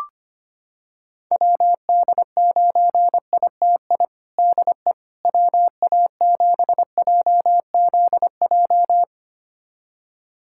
Morse code, heard as a keyed tone, about 25 wpm, 700 Hz. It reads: WD9ITI DE WA7JZJ
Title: USA Callsign DE Callsign - 6 Characters - 25wpm